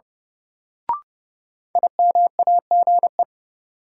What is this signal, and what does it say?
Morse code, heard as a keyed tone, about 30 wpm, 700 Hz. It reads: IMAGE
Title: Sentences from Top 700 Words - New Word Drill - 30wpm